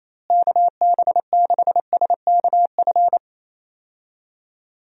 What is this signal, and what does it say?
Morse code, heard as a keyed tone, about 28 wpm, 700 Hz. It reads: KB6SKF
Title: USA Call Signs - 6 Characters - Repeated 2x - 28wpm